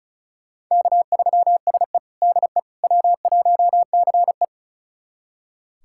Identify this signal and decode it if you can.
Morse code, heard as a keyed tone, about 36 wpm, 700 Hz. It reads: K3SE DE W1CE
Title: USA Callsign DE Callsign - 4 Characters - 35wpm